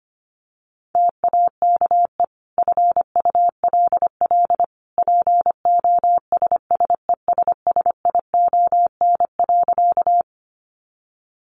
Morse code, heard as a keyed tone, about 25 wpm, 700 Hz. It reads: TAKE FULL POSSESSION.
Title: Sentences from Top 1400 Words 25wpm